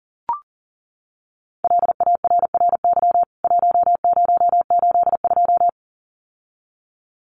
Morse code, heard as a keyed tone, about 40 wpm, 700 Hz. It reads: LARRY 1082
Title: CWT Exchange 40wpm